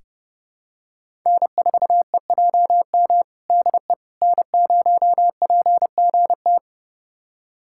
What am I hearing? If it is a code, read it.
Morse code, heard as a keyed tone, about 30 wpm, 700 Hz. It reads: N4EJM DE N0PGT
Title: USA Callsign DE Callsign - 5 Characters - 30wpm